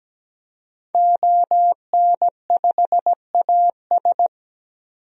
Morse code, heard as a keyed tone, about 17 wpm, 700 Hz. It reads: ON5AS
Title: DXCC-100 Call Signs - 5 Characters - 17wpm